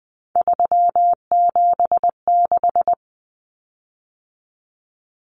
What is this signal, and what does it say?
Morse code, heard as a keyed tone, about 20 wpm, 700 Hz. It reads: 376
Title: Common Cut Numbers - 3-Digit 20wpm